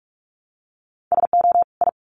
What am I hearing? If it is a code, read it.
Morse code, heard as a keyed tone, about 44 wpm, 700 Hz. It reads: SO I
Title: Sets of 2 Words 45wpm